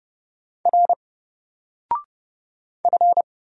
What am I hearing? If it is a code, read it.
Morse code, heard as a keyed tone, about 31 wpm, 700 Hz. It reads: R F
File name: Single-Letter-Number-30wpm